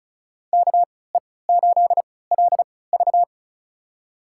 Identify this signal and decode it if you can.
Morse code, heard as a keyed tone, about 34 wpm, spaced as for 20 wpm, 700 Hz. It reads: KE8LV
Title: USA Call Signs - 5 Characters - 3x Character Spacing - 35wpm